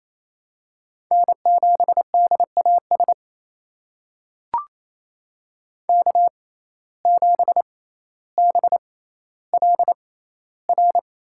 Morse code, heard as a keyed tone, about 28 wpm, spaced as for 14 wpm, 700 Hz. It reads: N7DAS K7BLR
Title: USA Call Signs - 5 Characters - 6x Character Spacing - 28wpm